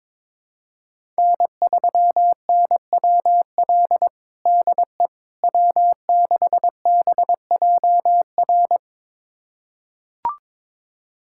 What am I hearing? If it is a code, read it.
Morse code, heard as a keyed tone, about 22 wpm, 700 Hz. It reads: N3NWL DE W6BJR
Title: USA Callsign DE Callsign - 5 Characters - 22wpm